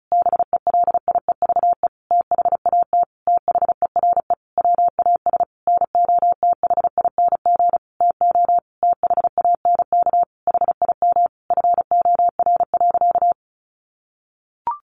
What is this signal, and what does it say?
Morse code, heard as a keyed tone, about 34 wpm, 700 Hz. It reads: BELIEVE THAT THERE WAS NOTHING TO THANK HIM FOR.
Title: Sentences from Top 1000 Words 35wpm